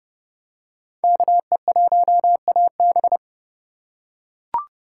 Morse code, heard as a keyed tone, about 30 wpm, 700 Hz. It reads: KE1AB